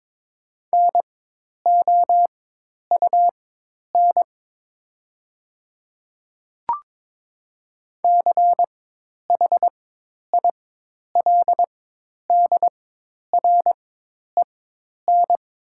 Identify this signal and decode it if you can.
Morse code, heard as a keyed tone, about 22 wpm, spaced as for 10 wpm, 700 Hz. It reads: NOUN CHILDREN
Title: 4x Character Spacing - Top 500 Words - 22wpm